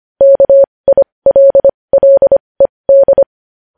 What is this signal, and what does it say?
Morse code, heard as a keyed tone, about 25 wpm, spaced as for 20 wpm, 550 Hz. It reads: KILLED
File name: podcast / News headlines-1x-25x20